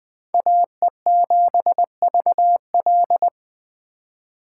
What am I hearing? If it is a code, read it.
Morse code, heard as a keyed tone, about 20 wpm, 700 Hz. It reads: AE7VL
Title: Speed Racing - USA 2x2 Call Signs 17wpm